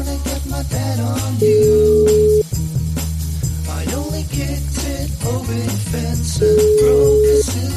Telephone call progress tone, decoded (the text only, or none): ringback tone